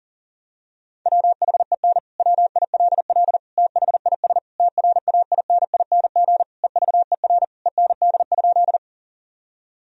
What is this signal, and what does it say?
Morse code, heard as a keyed tone, about 40 wpm, 700 Hz. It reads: WHEN WILL THIS TRAINING EVER END?
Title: Sentences from Top 1000 Words - Review Part 4 - 40wpm